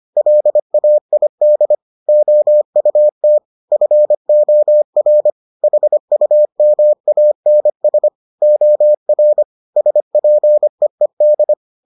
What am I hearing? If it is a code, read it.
Morse code, heard as a keyed tone, about 25 wpm, 600 Hz. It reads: LAID OUT FOR HUMANS OR SPEED